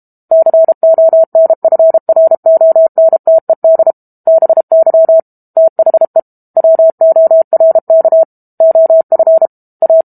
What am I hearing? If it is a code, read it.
Morse code, heard as a keyed tone, about 32 wpm, 650 Hz. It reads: CONFRONTED BY THE WORK OF A
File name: Book Two / 06-Chapter 6 - 32 WPM